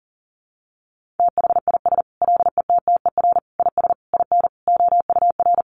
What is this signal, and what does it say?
Morse code, heard as a keyed tone, about 40 wpm, 700 Hz. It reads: THIS LETTER IS IN OUR
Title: Sentences from Top 1400 Words 40wpm